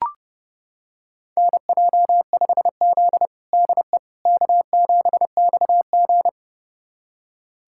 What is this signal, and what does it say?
Morse code, heard as a keyed tone, about 30 wpm, 700 Hz. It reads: NJ5Z DE K7XG
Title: USA Callsign DE Callsign - 4 Characters - 30wpm